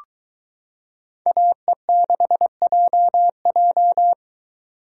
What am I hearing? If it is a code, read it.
Morse code, heard as a keyed tone, about 22 wpm, 700 Hz. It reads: AE6JJ